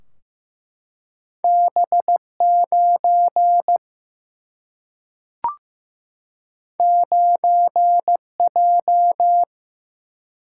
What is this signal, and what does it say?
Morse code, heard as a keyed tone, about 15 wpm, 700 Hz. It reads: B 9 9 J
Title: Single-Letter Single-Number 15wpm